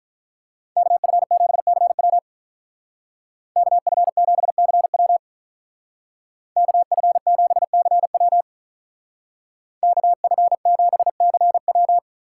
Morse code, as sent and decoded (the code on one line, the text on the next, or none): -.- ..-. --... -.-. .-- / -.- ..-. --... -.-. .-- / -.- ..-. --... -.-. .-- / -.- ..-. --... -.-. .--
KF7CW KF7CW KF7CW KF7CW